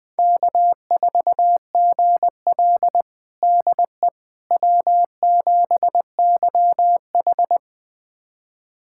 Morse code, heard as a keyed tone, about 20 wpm, 700 Hz. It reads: K4GL DE W7YH
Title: USA Callsign DE Callsign - 4 Characters - 20wpm